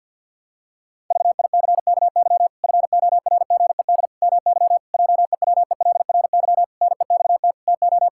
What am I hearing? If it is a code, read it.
Morse code, heard as a keyed tone, about 50 wpm, 700 Hz. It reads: VICKY FORGED MY JEWELRY NEXT TO
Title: Pangram Challenge v2 - 50wpm